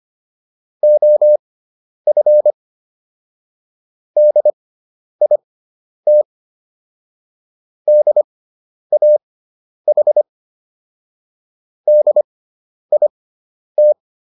Morse code, heard as a keyed tone, about 25 wpm, spaced as for 10 wpm, 600 Hz. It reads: OF DIT DAH DIT